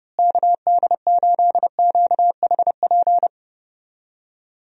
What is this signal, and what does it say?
Morse code, heard as a keyed tone, about 30 wpm, 700 Hz. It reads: KD8QHP
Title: USA Call Signs - 6 Characters - Repeated 2x - 30wpm